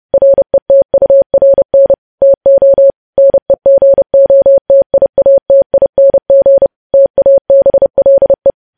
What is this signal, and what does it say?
Morse code, heard as a keyed tone, about 30 wpm, 550 Hz. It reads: RETURN TO NEGOTIATING TABLE
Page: news headlines in morse code at 30 wpm